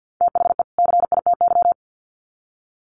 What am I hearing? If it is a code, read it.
Morse code, heard as a keyed tone, about 50 wpm, 700 Hz. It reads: THE CITY
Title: Sets of 2 Words 50wpm